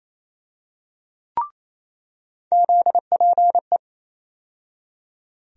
Morse code, heard as a keyed tone, about 28 wpm, 700 Hz. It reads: ZPE
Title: USA Call Sign Postfixes - 3 Letters 28wpm